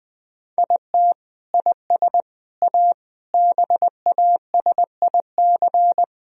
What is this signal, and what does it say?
Morse code, heard as a keyed tone, about 20 wpm, 700 Hz. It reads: IT IS A BASIC
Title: Sentences from Top 1000 Words - Review Part 3 - 20wpm